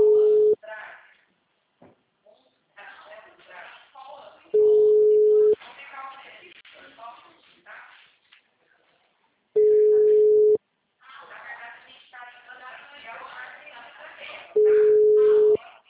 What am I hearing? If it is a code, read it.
Ringback tone (Europe, 425 Hz), with speech.
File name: out- -2011- - - .32111